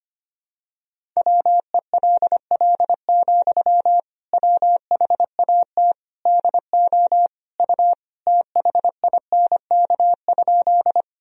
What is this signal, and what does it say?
Morse code, heard as a keyed tone, about 25 wpm, 700 Hz. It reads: WELL, WHAT DO U THINK?
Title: Sentences from Top 200 Words and Abbreviations 25wpm